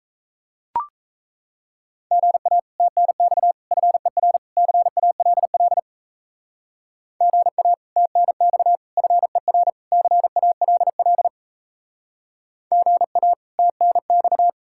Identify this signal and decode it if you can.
Morse code, heard as a keyed tone, about 35 wpm, 700 Hz. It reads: GA TNX FER CALL GA TNX FER CALL GA TNX